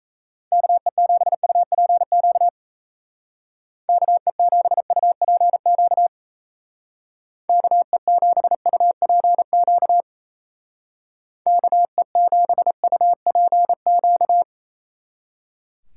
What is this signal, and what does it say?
Morse code, heard as a keyed tone, about 35 wpm, 700 Hz. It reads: KE7UPQ KE7UPQ KE7UPQ KE7UPQ
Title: Speed Racing - USA 2x3 Call Signs 28wpm